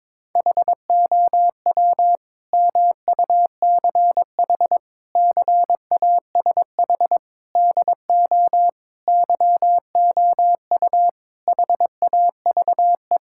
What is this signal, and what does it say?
Morse code, heard as a keyed tone, about 22 wpm, 700 Hz. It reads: HOW MUCH CASH DO YOU HAVE
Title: Sentences from Top 1000 Words - Review Part 5 - 22wpm